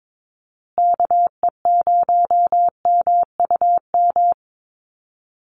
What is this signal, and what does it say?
Morse code, heard as a keyed tone, about 22 wpm, 700 Hz. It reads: KE0MUM
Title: USA Call Signs Repeated 3x 22wpm